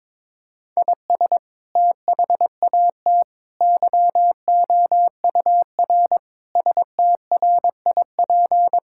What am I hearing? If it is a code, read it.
Morse code, heard as a keyed tone, about 22 wpm, 700 Hz. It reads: IS THAT YOUR STRIP
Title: Sentences from Top 1600 Words 22wpm